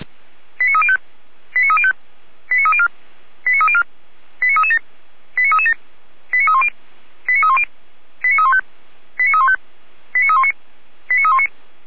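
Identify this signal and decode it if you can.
Five-tone selective call (ZVEI), heard as quick runs of five tones, tones of about 80 ms. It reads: 89305 89305 89304 89304 89337 89337 89310 89310 89315 89315 89319 89319
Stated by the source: ZVEI five-tone call received by SDR.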